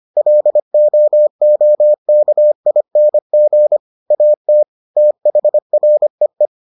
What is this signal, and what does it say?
Morse code, heard as a keyed tone, about 25 wpm, 600 Hz. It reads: LOOKING AT THREE